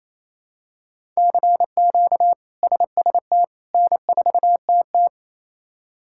Morse code, heard as a keyed tone, about 28 wpm, 700 Hz. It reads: CQ SST N4TT
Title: K1USN SST QSO Practice - 28wpm